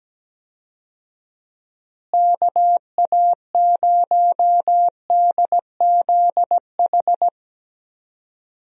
Morse code, heard as a keyed tone, about 17 wpm, 700 Hz. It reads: KA0DZH
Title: USA Call Signs - 6 Characters - Repeated 3x - 17wpm